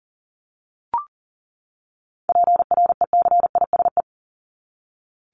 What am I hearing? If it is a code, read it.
Morse code, heard as a keyed tone, about 40 wpm, 700 Hz. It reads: PRECISE
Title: LY Suffix - Top 100 Words with Base Words 40wpm